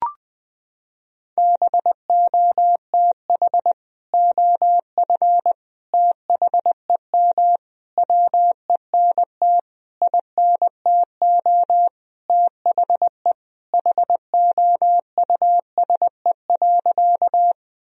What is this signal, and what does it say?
Morse code, heard as a keyed tone, about 20 wpm, 700 Hz. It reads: BOTH OF THEM WENT INTO THE HOUSE.
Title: Sentences from Top 1000 Words - Review Part 1 - 20wpm